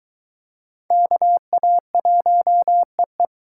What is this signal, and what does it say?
Morse code, heard as a keyed tone, about 22 wpm, 700 Hz. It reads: KA1EE